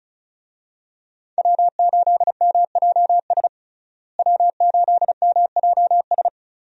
Morse code, heard as a keyed tone, about 36 wpm, 700 Hz. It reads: W8MJS W8MJS
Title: USA Call Signs - 5 Characters - Repeated 2x - 35wpm